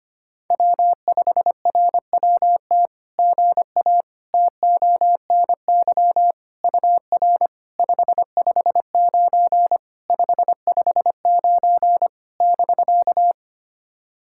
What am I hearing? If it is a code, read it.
Morse code, heard as a keyed tone, about 25 wpm, 700 Hz. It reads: W5RWT GA TONY UR 559 559 <BK>
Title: SOTA QSO Practice 25wpm